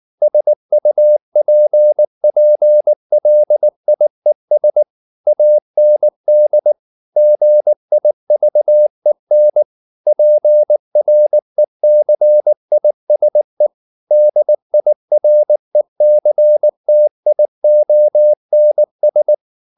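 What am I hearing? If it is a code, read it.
Morse code, heard as a keyed tone, about 19 wpm, 600 Hz. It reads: SUPPLIES AND GIVEN PRECISE DIRECTIONS
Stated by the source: Morse code at 19 wpm from CW med Gnister, reading Ken Follett's novel Never in English.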